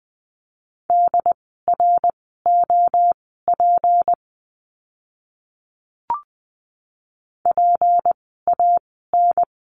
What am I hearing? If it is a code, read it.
Morse code, heard as a keyed tone, about 20 wpm, spaced as for 14 wpm, 700 Hz. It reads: DROP PAN
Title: 153 Lesson 25f - Letter p - Words 20wpm